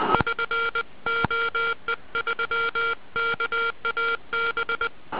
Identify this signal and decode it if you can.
Morse code, heard as a keyed tone, about 20 wpm, 1390 Hz. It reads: <SN> OE3KAB